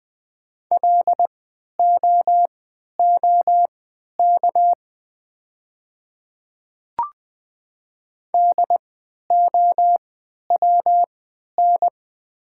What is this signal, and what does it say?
Morse code, heard as a keyed tone, about 20 wpm, spaced as for 11 wpm, 700 Hz. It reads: LOOK DOWN
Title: 3x Character Spacing - Top 100 Words - 20wpm